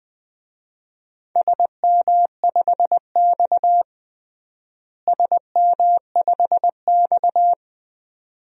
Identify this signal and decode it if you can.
Morse code, heard as a keyed tone, about 21 wpm, 700 Hz. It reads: SM5X SM5X
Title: DXCC-100 Call Signs - 4 Characters - Repeated 2x - 20wpm